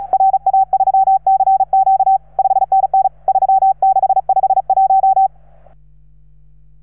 Morse code, heard as a keyed tone, about 36 wpm, 780 Hz. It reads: RA3CQ 5NN 3651